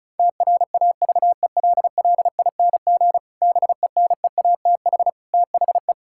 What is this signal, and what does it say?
Morse code, heard as a keyed tone, about 34 wpm, 700 Hz. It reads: TRAVELLING BENEATH THE
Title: Pangram Challenge v2 - 35wpm